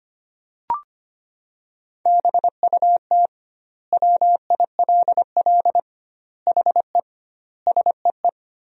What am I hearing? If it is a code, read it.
Morse code, heard as a keyed tone, about 25 wpm, 700 Hz. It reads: BUT WILL HE SEE
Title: 2x Word-Spacing - Sentences from Top 100 Words - 25wpm